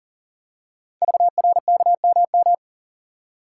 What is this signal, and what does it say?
Morse code, heard as a keyed tone, about 40 wpm, 700 Hz. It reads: VRKMM
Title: Code Groups - 5 Letters - 40wpm